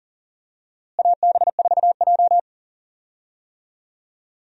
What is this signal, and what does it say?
Morse code, heard as a keyed tone, about 40 wpm, 700 Hz. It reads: AB4J
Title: USA Call Signs - 4 Characters - Repeated 3x - 40wpm